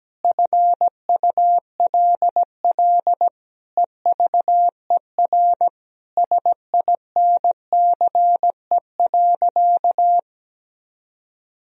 Morse code, heard as a keyed tone, about 17 wpm, 700 Hz. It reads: FULL EVER SINCE.
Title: Sentences from Top 1000 Words - Review Part 3 - 17wpm